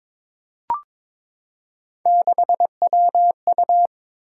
Morse code, heard as a keyed tone, about 22 wpm, 700 Hz. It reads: 6WU
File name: practice / 22wpm / Three Letter-Number Combinations - 22wpm